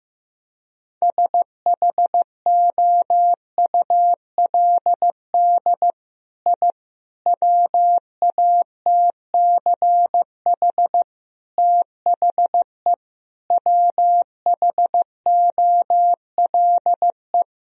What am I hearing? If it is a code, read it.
Morse code, heard as a keyed tone, about 16 wpm, 700 Hz. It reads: SHOULD I WATCH THE WHOLE
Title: Sentences from Top 1000 Words - Review Part 4 - 15wpm